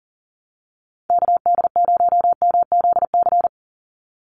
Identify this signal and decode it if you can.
Morse code, heard as a keyed tone, about 40 wpm, 700 Hz. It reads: KD0MZC
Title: USA Call Signs 40wpm